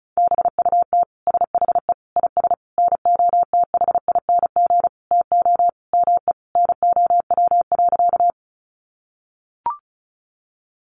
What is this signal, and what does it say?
Morse code, heard as a keyed tone, about 34 wpm, 700 Hz. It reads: BUT SHE IS NOTHING TO ME NOW.
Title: Sentences from Top 500 Words 35wpm